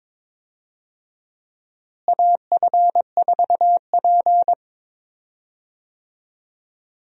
Morse code, heard as a keyed tone, about 22 wpm, 700 Hz. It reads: AF4P
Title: USA Call Signs - 4 Characters - Repeated 3x - 22wpm